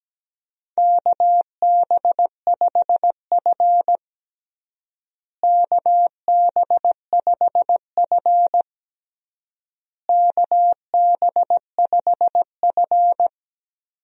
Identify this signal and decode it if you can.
Morse code, heard as a keyed tone, about 17 wpm, 700 Hz. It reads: KB5F KB5F KB5F
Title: USA Call Signs - 4 Characters - Repeated 3x - 17wpm